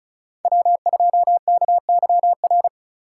Morse code, heard as a keyed tone, about 43 wpm, 700 Hz. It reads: W2KYR
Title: Speed Racing - USA 1x3 Call Signs 35wpm